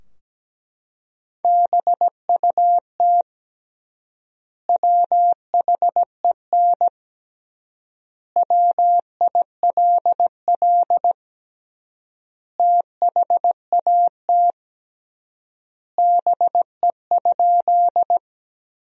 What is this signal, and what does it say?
Morse code, heard as a keyed tone, about 17 wpm, 700 Hz. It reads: BUT WHEN WILL THAT BE?
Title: 3x Word-Spacing - Sentences from Top 100 Words - 17wpm